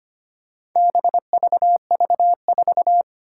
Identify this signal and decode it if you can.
Morse code, heard as a keyed tone, about 25 wpm, 700 Hz. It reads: BVV4
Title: Four Challenging Letter-Number Combinations 25wpm